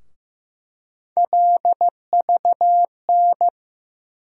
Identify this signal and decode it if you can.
Morse code, heard as a keyed tone, about 15 wpm, 700 Hz. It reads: LVN
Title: USA Call Sign Postfixes - 3 Letters 15wpm